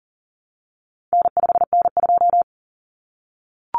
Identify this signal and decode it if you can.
Morse code, heard as a keyed tone, about 40 wpm, 700 Hz. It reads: N5N2
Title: Common Cut Numbers - 4-Digit 40wpm